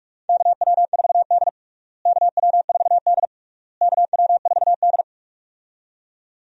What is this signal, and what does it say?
Morse code, heard as a keyed tone, about 46 wpm, 700 Hz. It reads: KW4D KW4D KW4D
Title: USA Call Signs - 4 Characters - Repeated 3x - 45wpm